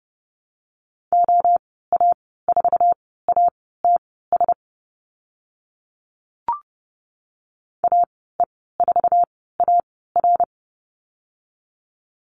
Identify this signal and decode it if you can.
Morse code, heard as a keyed tone, about 30 wpm, spaced as for 17 wpm, 700 Hz. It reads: OA4ATS AE4AR
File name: Lesson 14g - Number 2 - Call Signs 30wpm